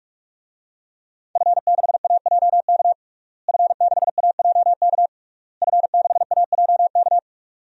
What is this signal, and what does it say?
Morse code, heard as a keyed tone, about 41 wpm, 700 Hz. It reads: F6AJK F6AJK F6AJK
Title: DXCC-100 Call Signs - 5 Characters - Repeated 3x - 45wpm